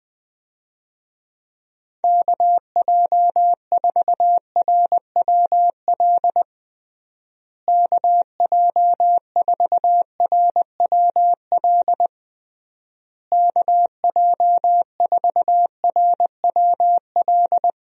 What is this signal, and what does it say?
Morse code, heard as a keyed tone, about 21 wpm, 700 Hz. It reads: KJ4RWL KJ4RWL KJ4RWL
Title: USA Call Signs - 6 Characters - Repeated 3x - 20wpm